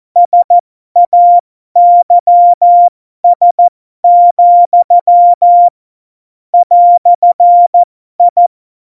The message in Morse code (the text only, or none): SAYS, "I